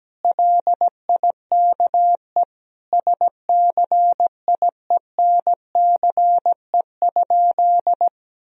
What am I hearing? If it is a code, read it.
Morse code, heard as a keyed tone, about 17 wpm, 700 Hz. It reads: LIKE SCIENCE?
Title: Sentences from Top 1000 Words - Review Part 2 - 17wpm